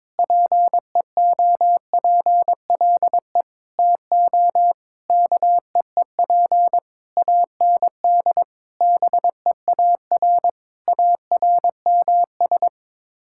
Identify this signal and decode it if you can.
Morse code, heard as a keyed tone, about 21 wpm, 700 Hz. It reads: PEOPLE TO KEEP AND BEAR ARMS
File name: Sets of 9 Words - 22wpm